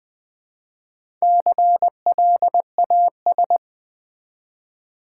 Morse code, heard as a keyed tone, about 21 wpm, 700 Hz. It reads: CLAS
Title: Build A Word - 1 Letter At A Time - 5-Letter Words - 20wpm